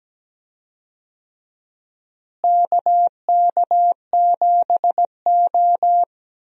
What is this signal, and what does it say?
Morse code, heard as a keyed tone, about 17 wpm, 700 Hz. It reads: KK7O
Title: USA Call Signs - 4 Characters - Repeated 3x - 17wpm